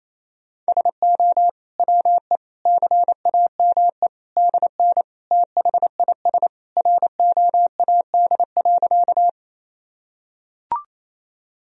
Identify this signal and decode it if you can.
Morse code, heard as a keyed tone, about 27 wpm, 700 Hz. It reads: SO WE CAME DN THIS ROAD.
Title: Sentences from Top 400 Words and Abbreviations 28wpm